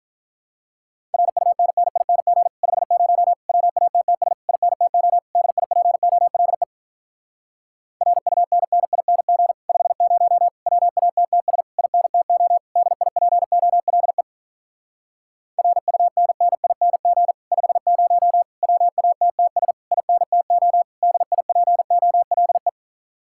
Morse code, heard as a keyed tone, about 48 wpm, 700 Hz. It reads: RUNNING 50 WATTS INTO DIPOLE RUNNING 50 WATTS INTO DIPOLE RUNNING 50 WATTS INTO DIPOLE